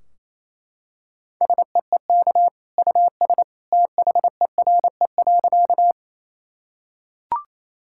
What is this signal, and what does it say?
Morse code, heard as a keyed tone, about 28 wpm, 700 Hz. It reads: SEEK US THERE.
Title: Sentences from Top 1600 Words 28wpm